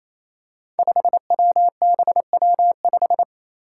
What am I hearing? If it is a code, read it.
Morse code, heard as a keyed tone, about 27 wpm, 700 Hz. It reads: 5WBW5
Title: Five Challenging Letter-Number Combinations 28wpm